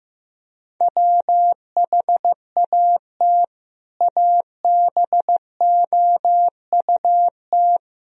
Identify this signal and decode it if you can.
Morse code, heard as a keyed tone, about 15 wpm, 700 Hz. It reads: WHAT ABOUT
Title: Sentences from Top 1000 Words - Review Part 1 - 15wpm